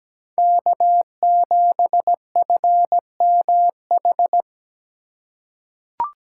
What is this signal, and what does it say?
Morse code, heard as a keyed tone, about 17 wpm, 700 Hz. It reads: K7FMH
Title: USA Call Signs - 5 Characters - 6x Character Spacing - 17wpm